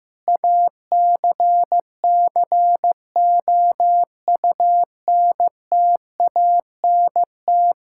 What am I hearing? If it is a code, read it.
Morse code, heard as a keyed tone, about 15 wpm, 700 Hz. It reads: ACCOUNTANT
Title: K1USN SST QSO Practice 15wpm